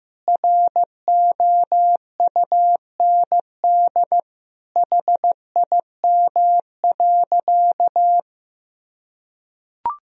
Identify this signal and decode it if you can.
Morse code, heard as a keyed tone, about 16 wpm, 700 Hz. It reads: ROUND HIM.
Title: Sentences from Top 1000 Words - Review Part 1 - 15wpm